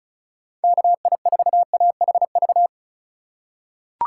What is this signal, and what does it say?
Morse code, heard as a keyed tone, about 36 wpm, 700 Hz. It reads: KI4AHV
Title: USA Call Signs - 6 Characters - 3x Character Spacing - 35wpm